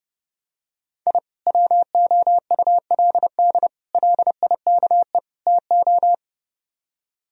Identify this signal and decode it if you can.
Morse code, heard as a keyed tone, about 32 wpm, 700 Hz. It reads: I WOULD LIKE TO